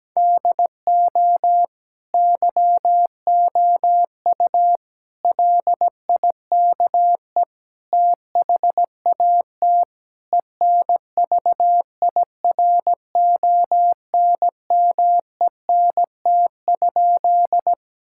Morse code, heard as a keyed tone, about 17 wpm, 700 Hz. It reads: DO YOU LIKE THAT ENVIRONMENT?
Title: Sentences from Top 1000 Words - Review Part 5 - 17wpm